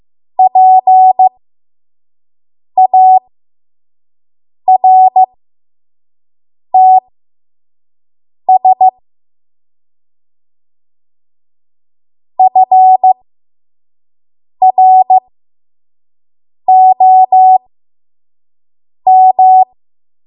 Morse code, heard as a keyed tone, about 15 wpm, spaced as for 5 wpm, 750 Hz. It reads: PARTS FROM
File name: Morse / 05WPM